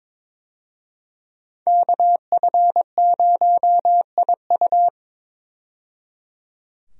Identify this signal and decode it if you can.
Morse code, heard as a keyed tone, about 22 wpm, 700 Hz. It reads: KF0IU
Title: USA Call Signs - 5 Characters - 22wpm